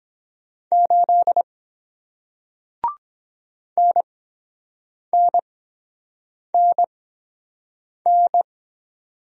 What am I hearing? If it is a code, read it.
Morse code, heard as a keyed tone, about 23 wpm, spaced as for 7 wpm, 700 Hz. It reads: 8 NNNN